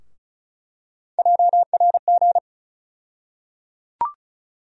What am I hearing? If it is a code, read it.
Morse code, heard as a keyed tone, about 34 wpm, 700 Hz. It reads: JRG
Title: USA Call Sign Postfixes - 3 Letters 35wpm